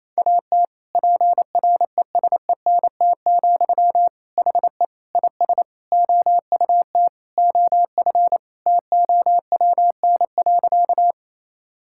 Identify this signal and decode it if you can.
Morse code, heard as a keyed tone, about 28 wpm, 700 Hz. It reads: AT PRESENT, HE IS OUT OF TOWN.
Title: Sentences from Top 1000 Words - Review Part 5 - 28wpm